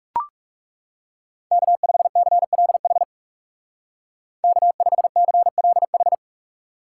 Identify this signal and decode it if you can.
Morse code, heard as a keyed tone, about 38 wpm, 700 Hz. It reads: K5CLH K5CLH